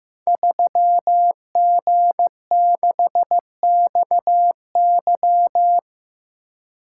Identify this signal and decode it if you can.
Morse code, heard as a keyed tone, about 15 wpm, 700 Hz. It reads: 3G6XY
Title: Code Groups - 5 Letter:Numbers - 15wpm